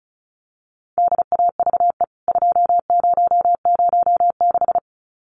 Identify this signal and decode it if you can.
Morse code, heard as a keyed tone, about 34 wpm, 700 Hz. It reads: DAVE 2006
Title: CWT Exchange 35wpm